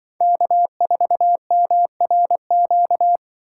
Morse code, heard as a keyed tone, about 24 wpm, 700 Hz. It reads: K4MRQ
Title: Speed Racing - USA 1x3 Call Signs 20wpm